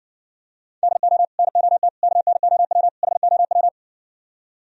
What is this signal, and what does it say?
Morse code, heard as a keyed tone, about 59 wpm, 700 Hz. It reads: DO NOT KNOW HOW